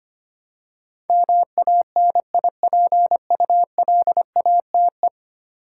Morse code, heard as a keyed tone, about 25 wpm, 700 Hz. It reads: MANIPULATE
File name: 10 Letter Words - First 3 Letters Given - 25wpm